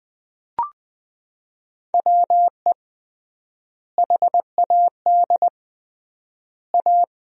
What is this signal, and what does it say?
Morse code, heard as a keyed tone, about 21 wpm, 700 Hz. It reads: WE HAD A
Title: 3x Word-Spacing - Sentences from Top 400 Words - 20wpm